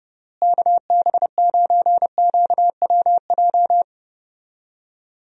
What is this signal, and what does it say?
Morse code, heard as a keyed tone, about 32 wpm, 700 Hz. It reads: KB9QWJ
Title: USA Call Signs - 6 Characters - Repeated 2x - 30wpm